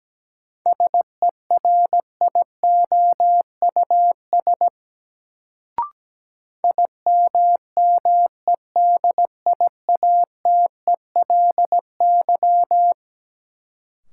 Morse code, heard as a keyed tone, about 17 wpm, 700 Hz. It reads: SERIOUS IMMEDIATELY
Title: Sentences from Top 1100 Words - New Word Drill - 17wpm